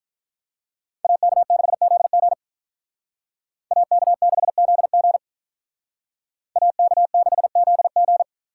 Morse code, heard as a keyed tone, about 46 wpm, 700 Hz. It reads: AK6ZG AK6ZG AK6ZG